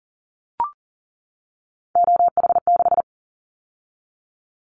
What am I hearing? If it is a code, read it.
Morse code, heard as a keyed tone, about 40 wpm, 700 Hz. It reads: OH6
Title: Callsign Prefixes for Top 50 DXCC - 40wpm